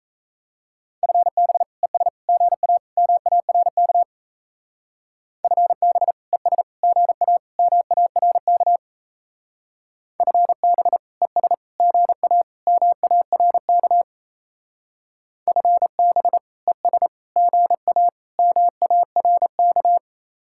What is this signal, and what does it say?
Morse code, heard as a keyed tone, about 34 wpm, 700 Hz. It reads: FB ES GA MARK FB ES GA MARK FB ES GA MARK FB ES GA MARK
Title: Speed-Racing Common QSO Phrases - 28wpm